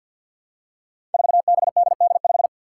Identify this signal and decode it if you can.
Morse code, heard as a keyed tone, about 47 wpm, 700 Hz. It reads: 4BDD5
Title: Five Challenging Letter-Number Combinations 50wpm